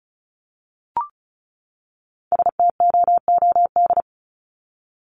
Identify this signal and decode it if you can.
Morse code, heard as a keyed tone, about 34 wpm, 700 Hz. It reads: STOOD
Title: Top 500 Words 35wpm